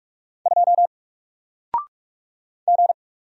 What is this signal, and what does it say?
Morse code, heard as a keyed tone, about 44 wpm, 700 Hz. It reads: 2 G
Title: Single Letter-Number Alt 45wpm